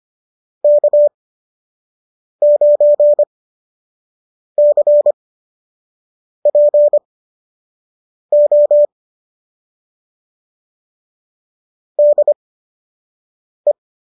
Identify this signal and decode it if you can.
Morse code, heard as a keyed tone, about 25 wpm, spaced as for 6 wpm, 590 Hz. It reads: K9CPO DE